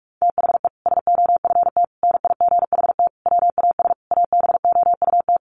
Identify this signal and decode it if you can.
Morse code, heard as a keyed tone, about 44 wpm, 700 Hz. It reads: THE SOFT NIGHT WAS ABOUT
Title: Sentences from Top 1500 Words 45wpm